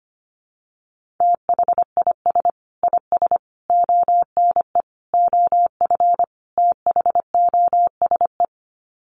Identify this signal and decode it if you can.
Morse code, heard as a keyed tone, about 25 wpm, 700 Hz. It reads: THIS IS ONE OF THOSE
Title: Sets of 5 Words Encore2 25wpm